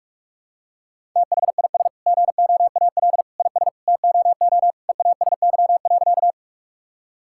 Morse code, heard as a keyed tone, about 46 wpm, 700 Hz. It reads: THIS GOAL IS TOO EASY.
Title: Sentences from Top 1000 Words - Review Part 5 - 45wpm